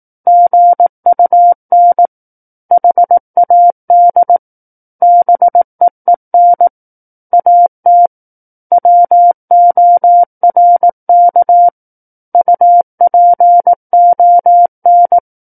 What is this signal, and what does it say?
Morse code, heard as a keyed tone, about 18 wpm, 700 Hz. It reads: GUN HAD BEEN AT WORK UPON
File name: Book One / 09-Chapter 9 - 18 WPM